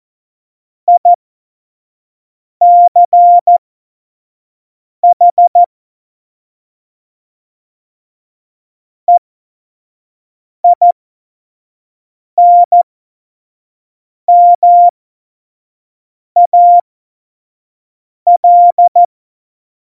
Morse code, heard as a keyed tone, about 14 wpm, spaced as for 5 wpm, 700 Hz. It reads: ICH EINMAL